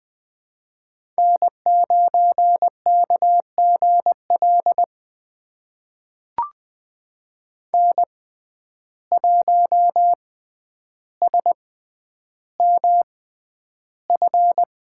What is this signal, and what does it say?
Morse code, heard as a keyed tone, about 20 wpm, spaced as for 10 wpm, 700 Hz. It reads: N9KGL N1SMF
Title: USA Call Signs - 6x Character Spacing - 20wpm